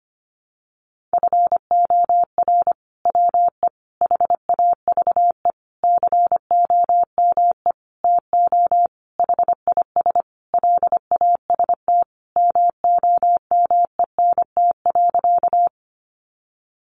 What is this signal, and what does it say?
Morse code, heard as a keyed tone, about 25 wpm, 700 Hz. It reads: FOR WE HAVE COME TO HIS LAST MOMENT.
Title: Sentences from Top 1000 Words 25wpm